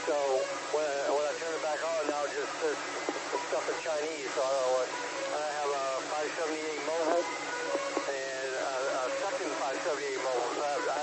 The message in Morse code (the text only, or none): none